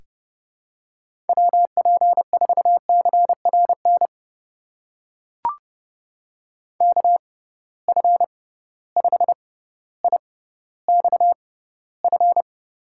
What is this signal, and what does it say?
Morse code, heard as a keyed tone, about 30 wpm, spaced as for 15 wpm, 700 Hz. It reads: WP4CRN KF5IXF
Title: USA Call Signs - 6 Characters - 6x Character Spacing - 30wpm